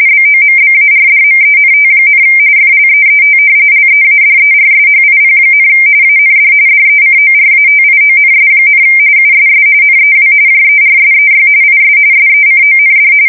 RTTY, about 45.45 baud. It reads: BOMBARDMENT, FLEET LEADERSHIP, AND DECORATIONS, USS NEW JERSEY IS SECOND
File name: fsk170 / battleshipnj